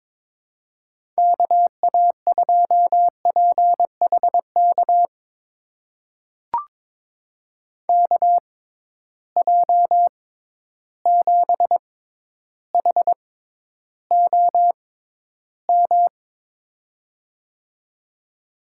Morse code, heard as a keyed tone, about 21 wpm, spaced as for 11 wpm, 700 Hz. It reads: KA2PHK KJ7HOM
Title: USA Call Signs - 6 Characters - 6x Character Spacing - 22wpm